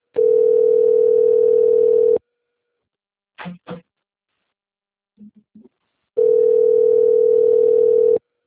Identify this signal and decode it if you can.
Ringback tone (North America), 440 + 480 Hz.
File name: rg-1600-unknown- - - .10920